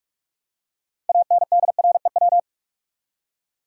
Morse code, heard as a keyed tone, about 46 wpm, 700 Hz. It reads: ANDREW
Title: USA Common Names - 45wpm